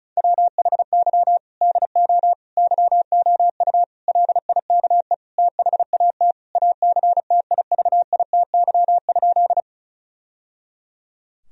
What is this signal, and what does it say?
Morse code, heard as a keyed tone, about 34 wpm, 700 Hz. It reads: WHY DO YOU LIKE THAT ACTIVITY?
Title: Sentences from Top 1000 Words - Review Part 4 - 35wpm